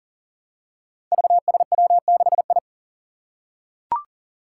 Morse code, heard as a keyed tone, about 42 wpm, 700 Hz. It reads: VSW6I
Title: Five Challenging Letter-Number Combinations 40wpm